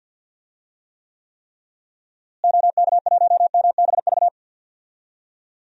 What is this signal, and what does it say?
Morse code, heard as a keyed tone, about 51 wpm, 700 Hz. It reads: OK1MBV
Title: DXCC-100 Call Signs - 6 Characters - Repeated 3x - 50wpm